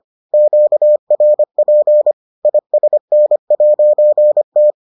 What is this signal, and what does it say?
Morse code, heard as a keyed tone, about 25 wpm, 600 Hz. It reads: QRP ISN'T